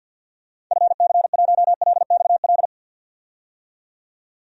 Morse code, heard as a keyed tone, about 49 wpm, 700 Hz. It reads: FC1LXL
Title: DXCC-100 Call Signs - 6 Characters - 50wpm